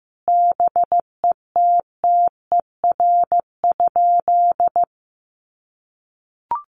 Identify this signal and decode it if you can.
Morse code, heard as a keyed tone, about 15 wpm, 700 Hz. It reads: BETTER?
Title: Sentences from Top 1200 Words 15wpm